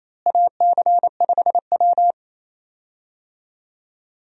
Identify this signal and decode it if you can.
Morse code, heard as a keyed tone, about 28 wpm, 700 Hz. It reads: AC5W
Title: USA Call Signs - 4 Characters - Repeated 2x - 28wpm